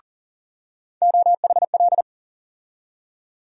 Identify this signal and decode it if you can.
Morse code, heard as a keyed tone, about 40 wpm, 700 Hz. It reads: OHL
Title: USA Call Sign Postfixes - 3 Letters 40wpm